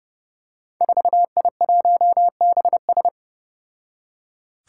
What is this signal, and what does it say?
Morse code, heard as a keyed tone, about 30 wpm, 700 Hz. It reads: 4I1BS
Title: Five Challenging Letter-Number Combinations 30wpm